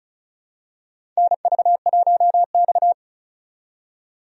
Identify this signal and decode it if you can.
Morse code, heard as a keyed tone, about 36 wpm, 700 Hz. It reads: NV1X
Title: USA Call Signs - 4 Characters - 35wpm